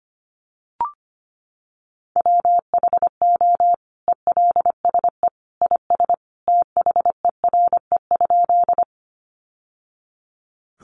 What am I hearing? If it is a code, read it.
Morse code, heard as a keyed tone, about 25 wpm, 700 Hz. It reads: WHO ELSE IS THERE?
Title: Sentences from Top 800 Words 25wpm